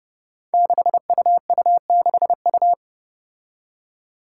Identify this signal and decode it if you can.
Morse code, heard as a keyed tone, about 30 wpm, 700 Hz. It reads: 6UU6U
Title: Five Challenging Letter-Number Combinations 30wpm